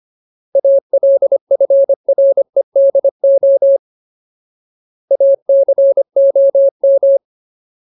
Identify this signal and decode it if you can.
Morse code, heard as a keyed tone, about 25 wpm, 550 Hz. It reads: ALFREDO ACOM
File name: Ultimate Word List / 25WPM Ultimate